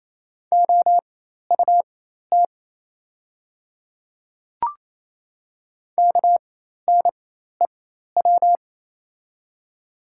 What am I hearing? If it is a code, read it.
Morse code, heard as a keyed tone, about 27 wpm, spaced as for 13 wpm, 700 Hz. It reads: OUT KNEW
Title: 4x Character Spacing - Top 500 Words - 28wpm